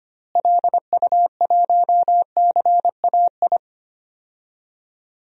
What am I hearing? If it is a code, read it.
Morse code, heard as a keyed tone, about 25 wpm, 700 Hz. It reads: LU1CAI
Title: DXCC-100 Call Signs - 6 Characters - Repeated 3x - 25wpm